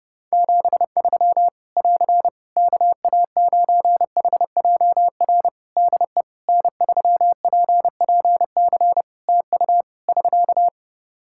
Morse code, heard as a keyed tone, about 30 wpm, 700 Hz. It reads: 73 <AR> KA9HJR DE N3PPC TU <SK>
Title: Practice QSO - Exchange 4 - Ending - 30wpm